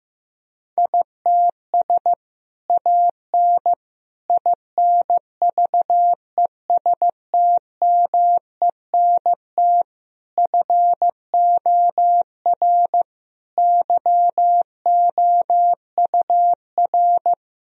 Morse code, heard as a keyed tone, about 15 wpm, 700 Hz. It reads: ITS AN INVESTMENT FOR YOUR
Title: Sentences from Top 1000 Words - Review Part 4 - 15wpm